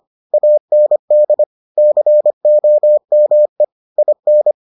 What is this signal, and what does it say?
Morse code, heard as a keyed tone, about 25 wpm, 600 Hz. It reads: AND COME IN